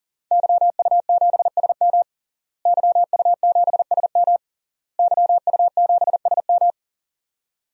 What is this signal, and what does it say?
Morse code, heard as a keyed tone, about 40 wpm, 700 Hz. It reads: YU7SM YU7SM YU7SM
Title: DXCC-100 Call Signs - 5 Characters - Repeated 3x - 40wpm